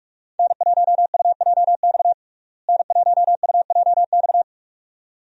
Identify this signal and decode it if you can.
Morse code, heard as a keyed tone, about 42 wpm, 700 Hz. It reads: N1UJX N1UJX
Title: USA Call Signs - 5 Characters - Repeated 2x - 45wpm